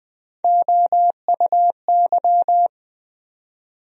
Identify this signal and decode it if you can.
Morse code, heard as a keyed tone, about 20 wpm, 700 Hz. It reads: OUY
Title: USA Call Sign Postfixes - 3 Letters 20wpm